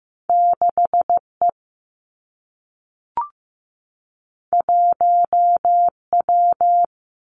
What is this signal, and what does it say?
Morse code, heard as a keyed tone, about 15 wpm, 700 Hz. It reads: 6 E 1 W
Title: Two Challenging Letter-Number Combinations 15wpm